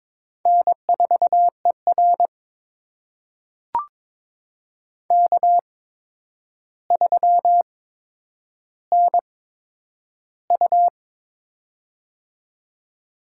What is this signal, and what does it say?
Morse code, heard as a keyed tone, about 22 wpm, spaced as for 9 wpm, 700 Hz. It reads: N4ER K3NU
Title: USA Call Signs - 4 Characters - 8x Character Spacing - 22wpm